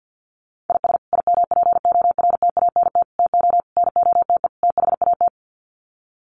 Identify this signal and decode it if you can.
Morse code, heard as a keyed tone, about 50 wpm, 700 Hz. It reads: IS IMPORTANT TO NOTE THAT
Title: Sets of 5 Words 50wpm